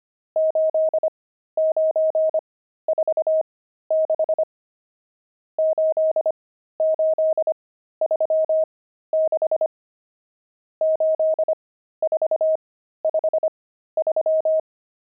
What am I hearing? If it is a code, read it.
Morse code, heard as a keyed tone, about 25 wpm, spaced as for 13 wpm, 630 Hz. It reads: 8946 8836 8453
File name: pangram-fs-13-cs-25